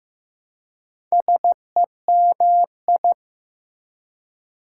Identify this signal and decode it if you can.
Morse code, heard as a keyed tone, about 15 wpm, 700 Hz. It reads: SEMI
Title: Prefixes and Suffixes 15wpm